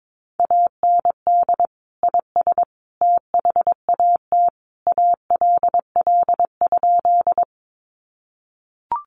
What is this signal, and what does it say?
Morse code, heard as a keyed tone, about 22 wpm, 700 Hz. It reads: AND IS THAT ALL?
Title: Sentences from Top 100 Words 22wpm